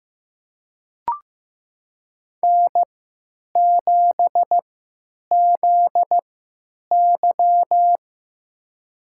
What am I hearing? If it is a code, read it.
Morse code, heard as a keyed tone, about 15 wpm, spaced as for 9 wpm, 700 Hz. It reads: N7ZY
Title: USA Call Signs - 4 Characters - 3x Character Spacing - 15wpm